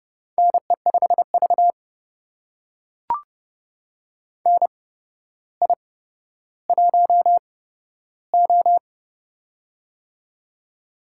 Morse code, heard as a keyed tone, about 30 wpm, spaced as for 13 wpm, 700 Hz. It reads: NE5V NI1O
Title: USA Call Signs - 4 Characters - 8x Character Spacing - 30wpm